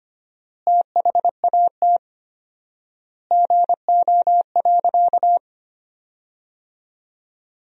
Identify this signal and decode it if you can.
Morse code, heard as a keyed tone, about 25 wpm, 700 Hz. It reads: THAT GO.
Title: 4x Word-Spacing - Sentences from Top 300 Words - 25wpm